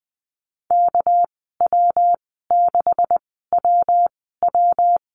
Morse code, heard as a keyed tone, about 20 wpm, spaced as for 14 wpm, 700 Hz. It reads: KW6WW
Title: 238 Lesson 38g - Letter q - Call Signs 20wpm